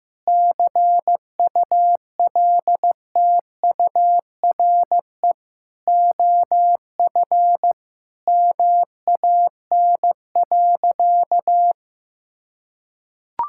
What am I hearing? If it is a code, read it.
Morse code, heard as a keyed tone, about 15 wpm, 700 Hz. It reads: CULTURE OF MAN.
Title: Sentences from Top 1000 Words - Review Part 6 - 15wpm